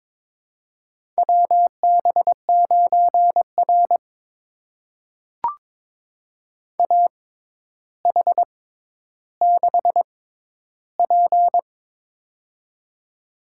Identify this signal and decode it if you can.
Morse code, heard as a keyed tone, about 22 wpm, spaced as for 11 wpm, 700 Hz. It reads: WB9R AH6P
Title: USA Call Signs - 4 Characters - 6x Character Spacing - 22wpm